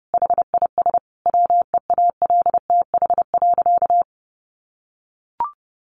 Morse code, heard as a keyed tone, about 30 wpm, 700 Hz. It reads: HIS WEALTH.
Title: Sentences from Top 1200 Words 30wpm